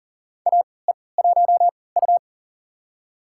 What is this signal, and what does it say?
Morse code, heard as a keyed tone, about 37 wpm, spaced as for 22 wpm, 700 Hz. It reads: AE1U
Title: USA Call Signs - 4 Characters - 3x Character Spacing - 40wpm